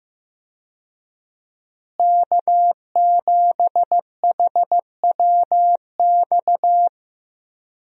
Morse code, heard as a keyed tone, about 15 wpm, 700 Hz. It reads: K7HWX
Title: USA Call Signs - 5 Characters - Repeated 3x - 15wpm